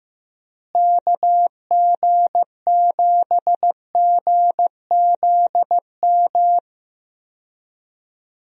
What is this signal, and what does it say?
Morse code, heard as a keyed tone, about 15 wpm, 700 Hz. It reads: KG7GZM
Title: USA Call Signs - 6 Characters - Repeated 2x - 15wpm